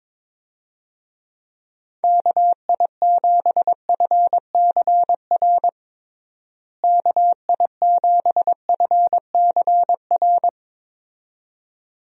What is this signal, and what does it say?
Morse code, heard as a keyed tone, about 21 wpm, 700 Hz. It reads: KI7FCR KI7FCR
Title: USA Call Signs - 6 Characters - Repeated 2x - 22wpm